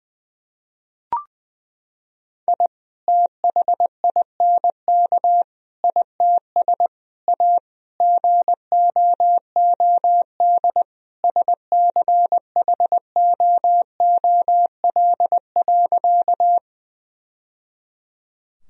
Morse code, heard as a keyed tone, about 20 wpm, 700 Hz. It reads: I THINK ITS A GOOD SCHOOL.
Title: Sentences from Top 1000 Words - Review Part 1 - 20wpm